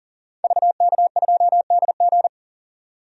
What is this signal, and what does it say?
Morse code, heard as a keyed tone, about 40 wpm, 700 Hz. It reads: VK2DG
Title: DXCC-100 Call Signs - 5 Characters - 40wpm